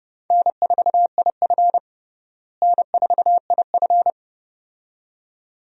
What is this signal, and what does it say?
Morse code, heard as a keyed tone, about 30 wpm, 700 Hz. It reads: N4IF N4IF
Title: USA Call Signs - 4 Characters - Repeated 2x - 30wpm